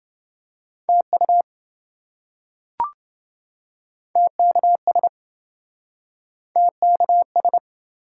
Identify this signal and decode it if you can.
Morse code, heard as a keyed tone, about 27 wpm, 700 Hz. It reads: TU TKS TKS